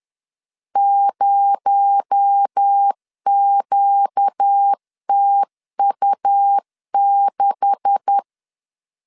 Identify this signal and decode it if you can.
Morse code, heard as a keyed tone, about 10 wpm, 800 Hz. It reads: 0QTU6
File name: LEZIONI / 10 SIGNS / 10C DOT 10wpm 2940